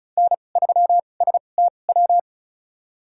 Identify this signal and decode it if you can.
Morse code, heard as a keyed tone, about 36 wpm, spaced as for 25 wpm, 700 Hz. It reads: N3STW
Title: USA Call Signs - 5 Characters - 2x Character Spacing - 35wpm